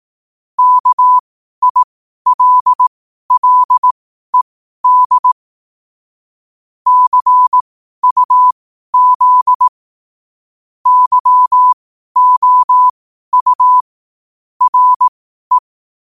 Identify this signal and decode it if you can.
Morse code, heard as a keyed tone, about 18 wpm, spaced as for 12 wpm, 1000 Hz. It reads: KILLED CUZ YOU RE
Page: Quotes for Sun, 27 Apr 2025 in Morse Code at 12 words per minute.